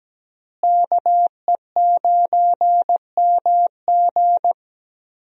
Morse code, heard as a keyed tone, about 17 wpm, 700 Hz. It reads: KE9MG
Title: USA Call Signs - 5 Characters - Repeated 3x - 17wpm